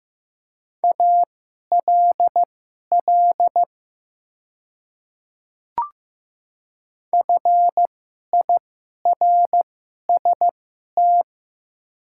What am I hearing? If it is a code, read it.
Morse code, heard as a keyed tone, about 15 wpm, spaced as for 11 wpm, 690 Hz. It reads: ALL FIRST
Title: 2x Character Spacing - Top 100 Words - 15wpm